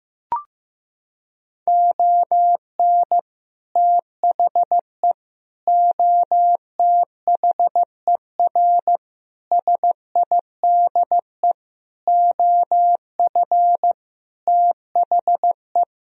Morse code, heard as a keyed tone, about 15 wpm, 700 Hz. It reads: ON THE OTHER SIDE OF THE
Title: Sentences from Top 1000 Words - Review Part 1 - 15wpm